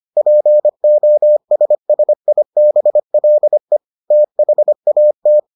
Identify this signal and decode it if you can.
Morse code, heard as a keyed tone, about 25 wpm, 600 Hz. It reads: POSSIBLE THAT